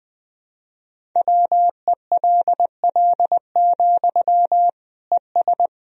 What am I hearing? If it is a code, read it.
Morse code, heard as a keyed tone, about 20 wpm, 700 Hz. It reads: WELL, ES
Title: Sentences from Top 200 Words and Abbreviations 20wpm